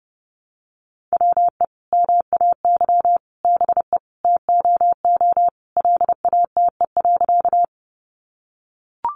Morse code, heard as a keyed tone, about 30 wpm, 700 Hz. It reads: WE MAY BE TOO LATE.
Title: Sentences from Top 300 Words 30wpm